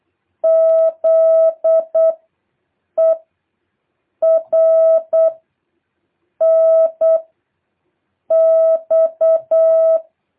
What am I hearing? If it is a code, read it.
Morse code, heard as a keyed tone, about 8 wpm, 650 Hz. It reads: Z E R N X
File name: jh8cbh / 2023cw